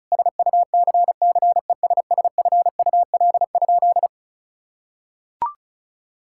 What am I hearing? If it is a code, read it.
Morse code, heard as a keyed tone, about 36 wpm, 700 Hz. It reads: SUCCESSFUL?
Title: Sentences from Top 1000 Words - Review Part 6 - 35wpm